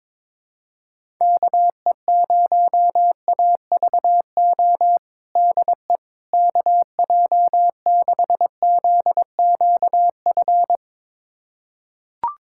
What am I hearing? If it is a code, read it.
Morse code, heard as a keyed tone, about 22 wpm, 700 Hz. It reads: KE0AVO DE KJ6ZQF
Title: USA Callsign DE Callsign 22wpm